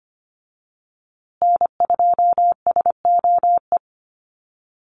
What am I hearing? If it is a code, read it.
Morse code, heard as a keyed tone, about 25 wpm, 700 Hz. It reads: N2SOE
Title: USA Call Signs Repeated 2x 25wpm